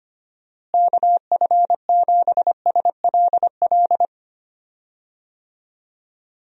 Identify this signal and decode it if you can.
Morse code, heard as a keyed tone, about 25 wpm, 700 Hz. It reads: KF7SLL
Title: USA Call Signs - 6 Characters - Repeated 2x - 25wpm